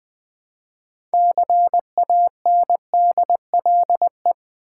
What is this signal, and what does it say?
Morse code, heard as a keyed tone, about 21 wpm, 700 Hz. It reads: CANDLE
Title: Six Letter Words - 20wpm